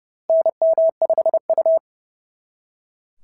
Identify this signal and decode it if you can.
Morse code, heard as a keyed tone, about 30 wpm, 660 Hz. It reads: NM5U
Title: SOTA QSO Practice 30wpm